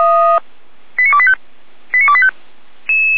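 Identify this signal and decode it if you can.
Five-tone selective call (ZVEI), heard as quick runs of five tones, tones of about 70 ms. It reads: 89285 89285
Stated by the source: ZVEI five-tone selective call received by SDR.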